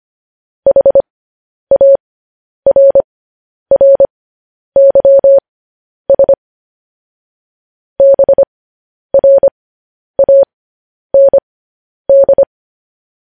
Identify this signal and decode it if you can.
Morse code, heard as a keyed tone, about 25 wpm, spaced as for 10 wpm, 550 Hz. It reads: HARRYS BRAND
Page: news-headlines-in-morse-code-at-10-wpm